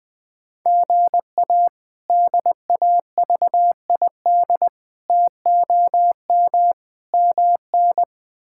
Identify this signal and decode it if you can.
Morse code, heard as a keyed tone, about 20 wpm, 700 Hz. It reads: GA DAVID TOM MN
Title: K1USN SST QSO Practice - 20wpm